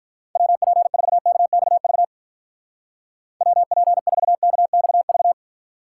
Morse code, heard as a keyed tone, about 44 wpm, 700 Hz. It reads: WP4KXV WP4KXV